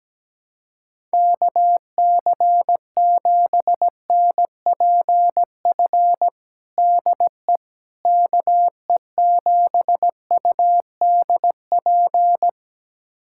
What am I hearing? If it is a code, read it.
Morse code, heard as a keyed tone, about 17 wpm, 700 Hz. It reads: KC7NPF DE KE7UDP
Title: USA Callsign DE Callsign - 6 Characters - 17wpm